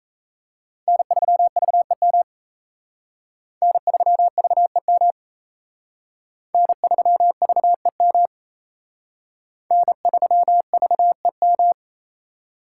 Morse code, heard as a keyed tone, about 34 wpm, 700 Hz. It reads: N3VEM N3VEM N3VEM N3VEM
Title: Speed Racing - USA 1x3 Call Signs 28wpm